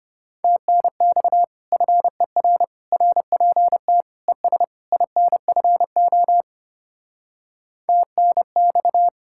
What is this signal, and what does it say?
Morse code, heard as a keyed tone, about 32 wpm, 700 Hz. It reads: TNX FER RPT ES INFO TNX